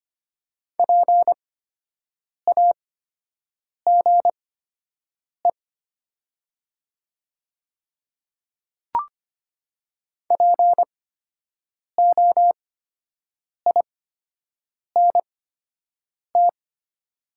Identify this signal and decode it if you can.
Morse code, heard as a keyed tone, about 25 wpm, spaced as for 7 wpm, 700 Hz. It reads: PAGE POINT
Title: 8x Character Spacing - Top 300 Words - 25wpm